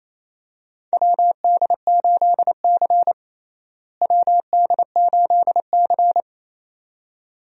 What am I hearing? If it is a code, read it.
Morse code, heard as a keyed tone, about 28 wpm, 700 Hz. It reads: WD8C WD8C
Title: USA Call Signs - 4 Characters - Repeated 2x - 28wpm